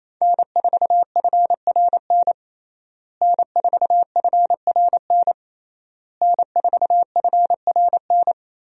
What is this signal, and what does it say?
Morse code, heard as a keyed tone, about 28 wpm, 700 Hz. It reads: N4FRN N4FRN N4FRN
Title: USA Call Signs - 5 Characters - Repeated 3x - 28wpm